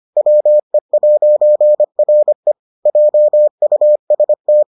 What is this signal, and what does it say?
Morse code, heard as a keyed tone, about 25 wpm, 600 Hz. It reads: WE'RE JUST